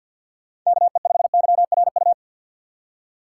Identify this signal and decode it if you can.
Morse code, heard as a keyed tone, about 52 wpm, 700 Hz. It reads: KE5YRU
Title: USA Call Signs - 6 Characters - 6x Character Spacing - 50wpm